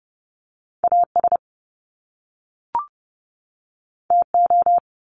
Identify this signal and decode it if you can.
Morse code, heard as a keyed tone, about 30 wpm, 700 Hz. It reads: AS TO
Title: Top Two Letter Combinations in Words 30wpm